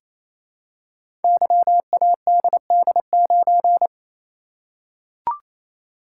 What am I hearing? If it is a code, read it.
Morse code, heard as a keyed tone, about 27 wpm, 700 Hz. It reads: YADD9
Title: Code Groups - 5 Letter:Numbers - 28wpm